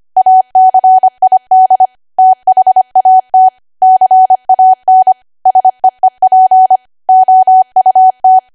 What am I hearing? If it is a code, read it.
Morse code, heard as a keyed tone, about 25 wpm, 750 Hz. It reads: ACID THAT CAN SEEP OUT
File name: Archive / 25 WPM / 25WPM